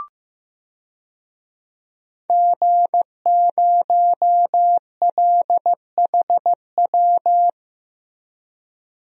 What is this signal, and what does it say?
Morse code, heard as a keyed tone, about 15 wpm, 700 Hz. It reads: G0LHW
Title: DXCC-100 Call Signs - 5 Characters - 15wpm